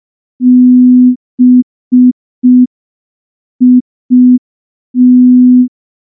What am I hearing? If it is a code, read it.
Morse code, heard as a keyed tone, about 5 wpm, 250 Hz. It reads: BIT
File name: morse / morse-training-05WPM